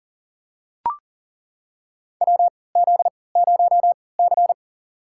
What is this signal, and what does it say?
Morse code, heard as a keyed tone, about 40 wpm, spaced as for 23 wpm, 700 Hz. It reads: WZ0C
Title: USA Call Signs - 4 Characters - 3x Character Spacing - 40wpm